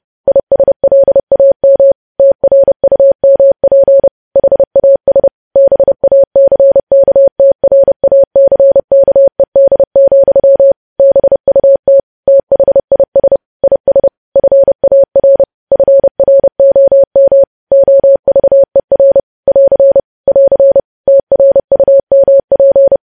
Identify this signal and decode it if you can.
Morse code, heard as a keyed tone, about 30 wpm, 550 Hz. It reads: ISLAM TRUMP HAS BACKTRACKED, BUT THIS IS FAR FROM OVER <AR> <AR> TRUMP
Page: news headlines in morse code at 30 wpm